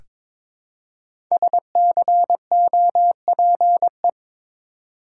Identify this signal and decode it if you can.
Morse code, heard as a keyed tone, about 22 wpm, 700 Hz. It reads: SCOPE
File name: 5 Letter Words - First 3 Letters Given - 22wpm